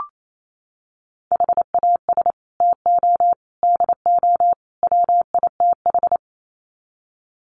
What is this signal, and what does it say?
Morse code, heard as a keyed tone, about 28 wpm, 700 Hz. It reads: HAS TO DO WITH
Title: Sets of 4 Words 28wpm